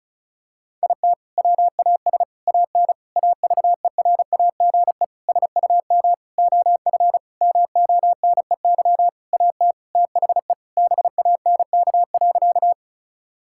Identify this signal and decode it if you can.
Morse code, heard as a keyed tone, about 34 wpm, 700 Hz. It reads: IT WAS AN AVERAGE SUM OF MONEY AT THE BANK.
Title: Sentences from Top 1000 Words - Review Part 6 - 35wpm